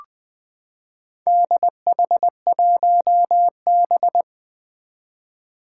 Morse code, heard as a keyed tone, about 20 wpm, 700 Hz. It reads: DH1B
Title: Four Challenging Letter-Number Combinations 20wpm